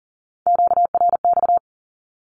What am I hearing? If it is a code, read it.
Morse code, heard as a keyed tone, about 38 wpm, 700 Hz. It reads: QRX
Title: Q-Codes with Spoken Meaning - 40wpm